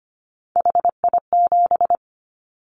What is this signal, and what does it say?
Morse code, heard as a keyed tone, about 25 wpm, 700 Hz. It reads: HI7
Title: Callsign Prefixes for Top 100 DXCC - 25wpm